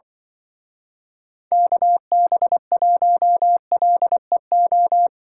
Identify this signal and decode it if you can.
Morse code, heard as a keyed tone, about 25 wpm, 700 Hz. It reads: KB1LEO
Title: Speed Racing - USA 2x3 Call Signs 20wpm